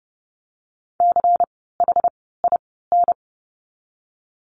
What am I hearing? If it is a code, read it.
Morse code, heard as a keyed tone, about 30 wpm, 700 Hz. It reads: C H I N
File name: Lesson 16f - Letter c - Words 30wpm